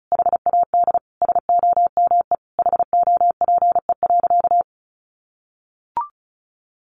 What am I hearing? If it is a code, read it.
Morse code, heard as a keyed tone, about 34 wpm, 700 Hz. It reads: HAD SOME HOPE.
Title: Sentences from Top 1000 Words 35wpm